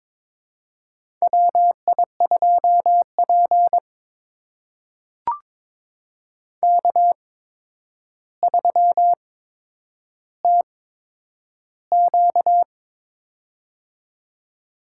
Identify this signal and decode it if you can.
Morse code, heard as a keyed tone, about 22 wpm, spaced as for 9 wpm, 700 Hz. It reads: WI2P K3TQ
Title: USA Call Signs - 4 Characters - 8x Character Spacing - 22wpm